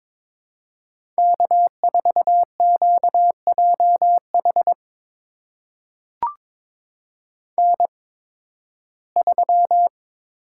Morse code, heard as a keyed tone, about 22 wpm, spaced as for 14 wpm, 700 Hz. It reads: K4QJH N3
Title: USA Call Signs - 5 Characters - 8x Character Spacing - 22wpm